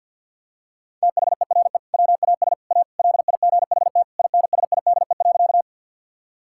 Morse code, heard as a keyed tone, about 50 wpm, 700 Hz. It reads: THERE WAS A LIGHT INSIDE.
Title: Sentences from Top 1000 Words - Review Part 6 - 50wpm